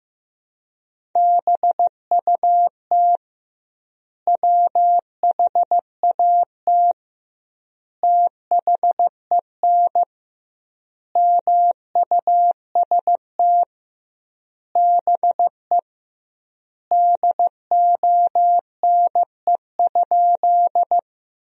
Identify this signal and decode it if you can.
Morse code, heard as a keyed tone, about 15 wpm, 700 Hz. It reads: BUT WHAT THEN MUST BE DONE?
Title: 2x Word-Spacing - Sentences from Top 500 Words - 15wpm